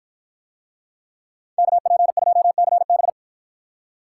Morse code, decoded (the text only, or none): KP2CB